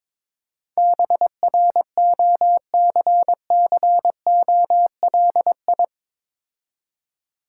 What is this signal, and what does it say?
Morse code, heard as a keyed tone, about 23 wpm, 690 Hz. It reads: BROCCOLI
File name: Difficult Words with Hints - 22wpm